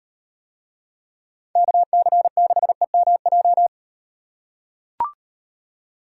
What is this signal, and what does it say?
Morse code, heard as a keyed tone, about 36 wpm, 700 Hz. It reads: KC6EMJ